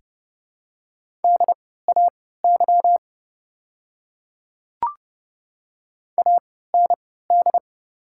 Morse code, heard as a keyed tone, about 30 wpm, spaced as for 17 wpm, 700 Hz. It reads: DAY AND
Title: 3x Character Spacing - Top 100 Words - 30wpm